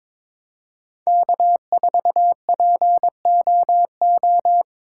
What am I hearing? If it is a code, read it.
Morse code, heard as a keyed tone, about 22 wpm, 700 Hz. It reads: K4POO
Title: USA Call Signs - 5 Characters - 3x Character Spacing - 22wpm